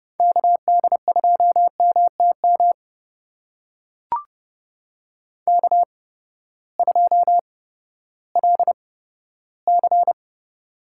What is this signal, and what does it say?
Morse code, heard as a keyed tone, about 30 wpm, spaced as for 15 wpm, 700 Hz. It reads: KD2MTM K2LC
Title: USA Call Signs - 8x Character Spacing - 30wpm